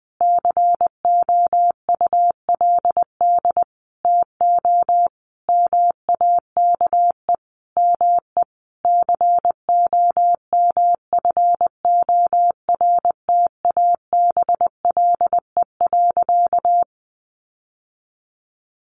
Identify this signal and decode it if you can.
Morse code, heard as a keyed tone, about 20 wpm, 700 Hz. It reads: COULD TO MAKE ME COMFORTABLE.
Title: Sentences from Top 1500 Words 20wpm